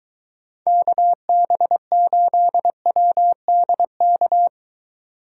Morse code, heard as a keyed tone, about 23 wpm, 700 Hz. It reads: KB8WDK